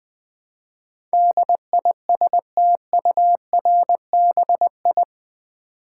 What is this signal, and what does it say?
Morse code, heard as a keyed tone, about 20 wpm, 700 Hz. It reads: DISTURBI
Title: Build A Word - 4 Letters At A Time - 10-Letter Words - 20wpm